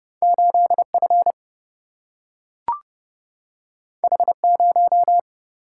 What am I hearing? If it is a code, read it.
Morse code, heard as a keyed tone, about 30 wpm, 700 Hz. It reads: 8F H0
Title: Single-Letter Single-Number 30wpm